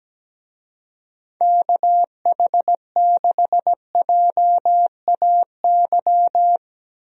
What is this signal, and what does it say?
Morse code, heard as a keyed tone, about 17 wpm, 700 Hz. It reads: KH6JAY
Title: DXCC-100 Call Signs - 6 Characters - 17wpm